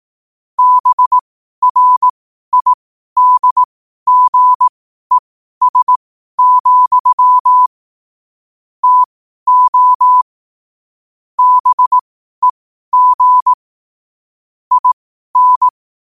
Morse code, heard as a keyed tone, about 18 wpm, spaced as for 12 wpm, 1000 Hz. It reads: BRIDGES, TO BEG IN
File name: qotdfiles / 12wpm